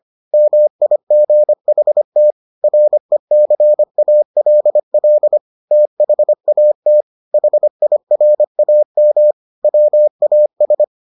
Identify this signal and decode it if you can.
Morse code, heard as a keyed tone, about 25 wpm, 600 Hz. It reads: MIGHT RECALL THAT HIRAM WAS